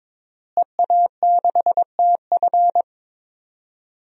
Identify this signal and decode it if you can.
Morse code, heard as a keyed tone, about 22 wpm, 700 Hz. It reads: EA6TF
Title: DXCC-100 Call Signs - 5 Characters - Repeated 3x - 22wpm